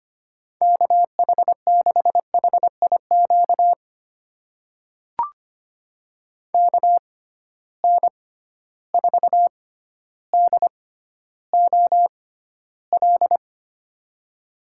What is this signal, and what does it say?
Morse code, heard as a keyed tone, about 25 wpm, spaced as for 13 wpm, 700 Hz. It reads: KH6HIQ KN4DOL
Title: USA Call Signs - 6 Characters - 6x Character Spacing - 25wpm